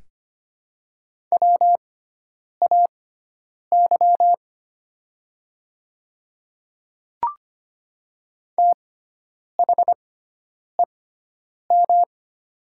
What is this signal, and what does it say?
Morse code, heard as a keyed tone, about 25 wpm, spaced as for 9 wpm, 700 Hz. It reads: WAY THEM
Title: 6x Character Spacing - Top 100 Words - 25wpm